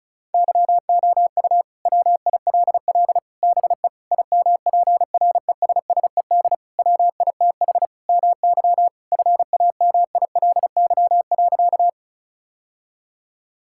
Morse code, heard as a keyed tone, about 36 wpm, 700 Hz. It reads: YOU WILL BE IMPRESSED WITH MY FAMILY.
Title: Sentences from Top 1600 Words 35wpm